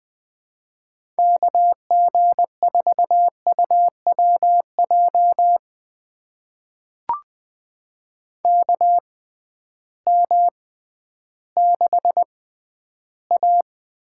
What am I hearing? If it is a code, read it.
Morse code, heard as a keyed tone, about 20 wpm, spaced as for 12 wpm, 700 Hz. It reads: KG4UWJ KM6A
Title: USA Call Signs - 6 Characters - 6x Character Spacing - 20wpm